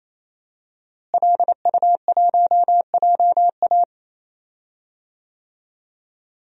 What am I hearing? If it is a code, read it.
Morse code, heard as a keyed tone, about 28 wpm, 690 Hz. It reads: LU1JA
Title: DXCC-100 Call Signs - 5 Characters - Repeated 3x - 28wpm